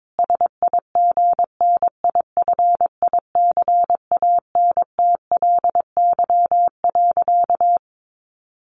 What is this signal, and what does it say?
Morse code, heard as a keyed tone, about 22 wpm, 700 Hz. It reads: SIGNIFICANTLY.
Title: Sentences from Top 1500 Words 22wpm